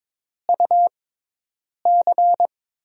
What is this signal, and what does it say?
Morse code, heard as a keyed tone, about 22 wpm, spaced as for 8 wpm, 700 Hz. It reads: UC
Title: USA Call Signs - 6x Character Spacing - 22wpm